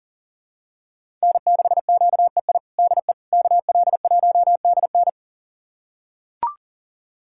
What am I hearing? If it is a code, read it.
Morse code, heard as a keyed tone, about 37 wpm, 700 Hz. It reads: N6QEI DE KL1DN
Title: USA Callsign DE Callsign - 5 Characters - 40wpm